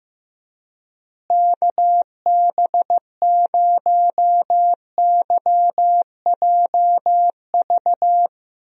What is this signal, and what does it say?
Morse code, heard as a keyed tone, about 15 wpm, 700 Hz. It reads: KB0YJV
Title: USA Call Signs - 6 Characters - Repeated 2x - 15wpm